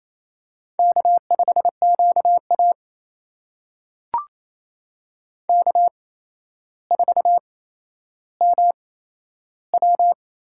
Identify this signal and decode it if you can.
Morse code, heard as a keyed tone, about 28 wpm, spaced as for 12 wpm, 700 Hz. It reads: K5QA K4MW
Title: USA Call Signs - 4 Characters - 8x Character Spacing - 28wpm